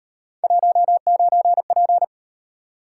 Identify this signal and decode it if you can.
Morse code, heard as a keyed tone, about 37 wpm, 700 Hz. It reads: 19P